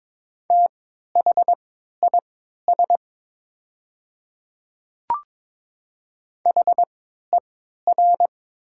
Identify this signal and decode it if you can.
Morse code, heard as a keyed tone, about 22 wpm, spaced as for 13 wpm, 700 Hz. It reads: THIS HER
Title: 3x Character Spacing - Top 100 Words - 22wpm